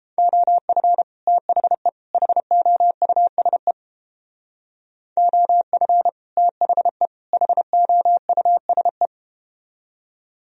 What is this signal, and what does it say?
Morse code, heard as a keyed tone, about 31 wpm, 700 Hz. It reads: OF THE HOUSE OF THE HOUSE